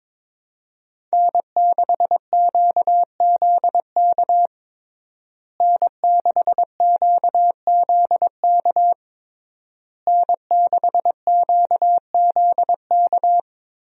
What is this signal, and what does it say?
Morse code, heard as a keyed tone, about 22 wpm, 700 Hz. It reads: N6QZK N6QZK N6QZK
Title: USA Call Signs - 5 Characters - Repeated 3x - 22wpm